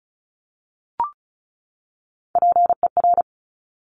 Morse code, heard as a keyed tone, about 36 wpm, 700 Hz. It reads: PER
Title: Top Three Letter Combinations in Words 35wpm